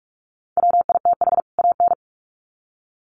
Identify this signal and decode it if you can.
Morse code, heard as a keyed tone, about 44 wpm, 700 Hz. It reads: WITH AN
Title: Sets of 2 Words 45wpm